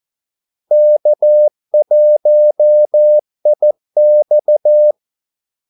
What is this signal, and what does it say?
Morse code, heard as a keyed tone, about 14 wpm, 600 Hz. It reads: K1IX
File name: Callsigns-14wpm